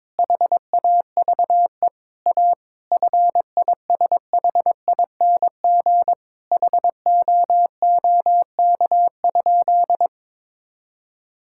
Morse code, heard as a keyed tone, about 22 wpm, 700 Hz. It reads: HAVE A FISHING HOOK?
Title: Sentences from Top 1600 Words 22wpm